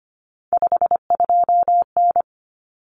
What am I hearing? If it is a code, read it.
Morse code, heard as a keyed tone, about 25 wpm, 700 Hz. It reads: 52N
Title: Common Cut Numbers - 3-Digit 25wpm